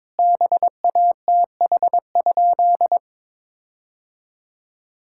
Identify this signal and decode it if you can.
Morse code, heard as a keyed tone, about 22 wpm, 700 Hz. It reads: BATH?
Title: Sentences from Top 1600 Words 22wpm